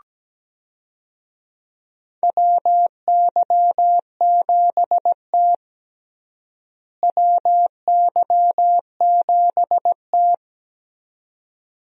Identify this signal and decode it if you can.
Morse code, heard as a keyed tone, about 17 wpm, 700 Hz. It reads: WY7T WY7T
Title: USA Call Signs - 4 Characters - Repeated 2x - 17wpm